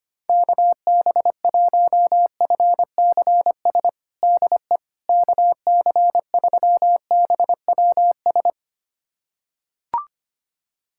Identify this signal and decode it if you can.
Morse code, heard as a keyed tone, about 25 wpm, 700 Hz. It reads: KB1FCS DE KC3BWS
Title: USA Callsign DE Callsign 25wpm